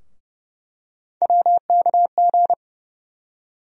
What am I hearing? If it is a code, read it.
Morse code, heard as a keyed tone, about 31 wpm, 700 Hz. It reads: WKG
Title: USA Call Sign Postfixes - 3 Letters 30wpm